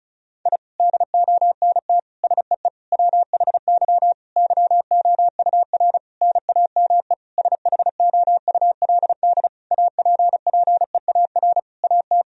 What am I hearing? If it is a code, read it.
Morse code, heard as a keyed tone, about 36 wpm, 700 Hz. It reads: I DONT SEE WHY YOUR NAME SHOULD APPEAR AT
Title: Sentences from Top 1000 Words - Review Part 2 - 35wpm